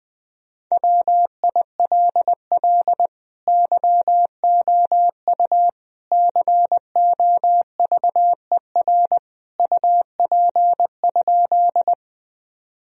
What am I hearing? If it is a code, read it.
Morse code, heard as a keyed tone, about 21 wpm, 700 Hz. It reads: WILL YOU COVER UP?
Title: Sentences from Top 1000 Words - Review Part 1 - 20wpm